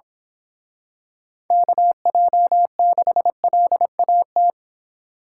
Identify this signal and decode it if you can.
Morse code, heard as a keyed tone, about 26 wpm, 700 Hz. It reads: KJ6LAT
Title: Speed Racing - USA 2x3 Call Signs 22wpm